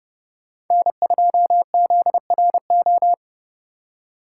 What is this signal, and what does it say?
Morse code, heard as a keyed tone, about 29 wpm, 700 Hz. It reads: N2ZRO